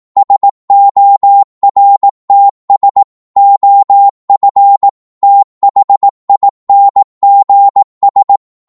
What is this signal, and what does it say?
Morse code, heard as a keyed tone, about 18 wpm, 800 Hz. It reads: SORTS OF THINGS